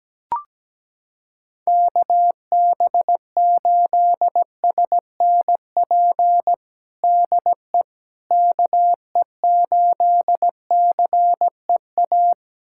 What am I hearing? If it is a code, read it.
Morse code, heard as a keyed tone, about 17 wpm, 700 Hz. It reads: KB8SNP DE KE8CEA
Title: USA Callsign DE Callsign - 6 Characters - 17wpm